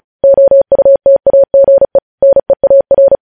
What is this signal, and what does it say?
Morse code, heard as a keyed tone, about 36 wpm, 550 Hz. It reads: OUTAGE NEAR
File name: podcast / News headlines-1x-35x35